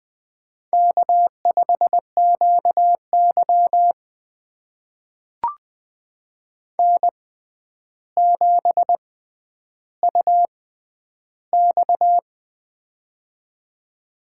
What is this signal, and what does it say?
Morse code, heard as a keyed tone, about 20 wpm, spaced as for 10 wpm, 700 Hz. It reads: K5QY N7UX
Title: USA Call Signs - 4 Characters - 6x Character Spacing - 20wpm